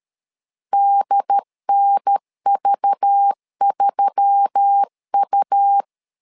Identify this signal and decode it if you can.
Morse code, heard as a keyed tone, about 13 wpm, 800 Hz. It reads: DNV3U
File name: LEZIONI / 5 LBVC / 5 LBV 12wpm 1855